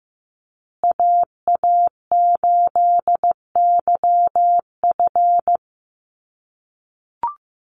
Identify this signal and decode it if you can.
Morse code, heard as a keyed tone, about 15 wpm, 700 Hz. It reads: AA8YF
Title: USA Call Signs Repeated 3x 15wpm